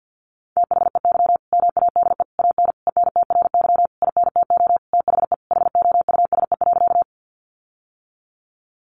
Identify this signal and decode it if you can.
Morse code, heard as a keyed tone, about 48 wpm, 700 Hz. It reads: THEY MADE AN ENTRY INTO THE HOUSE.
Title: Sentences from Top 1500 Words 50wpm